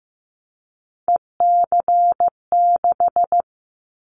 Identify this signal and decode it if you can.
Morse code, heard as a keyed tone, about 15 wpm, 700 Hz. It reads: EC6
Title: Callsign Prefixes for Top 50 DXCC - 15wpm